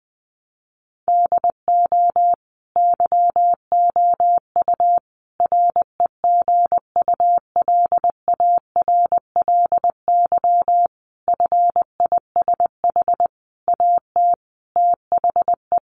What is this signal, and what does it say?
Morse code, heard as a keyed tone, about 20 wpm, 700 Hz. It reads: DO YOU REGULARLY FISH AT THE
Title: Sentences from Top 1400 Words 20wpm